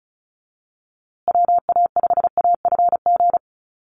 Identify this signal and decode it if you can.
Morse code, heard as a keyed tone, about 34 wpm, 700 Hz. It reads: WA5AFG
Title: USA Call Signs Repeated 2x 35wpm